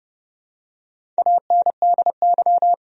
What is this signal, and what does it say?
Morse code, heard as a keyed tone, about 32 wpm, 700 Hz. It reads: ANDY
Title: USA Common Names - Max 5 Letters - 30wpm